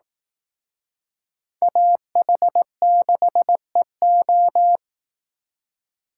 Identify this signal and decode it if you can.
Morse code, heard as a keyed tone, about 18 wpm, 700 Hz. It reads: AH6EO
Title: Speed Racing - USA Call Signs 15wpm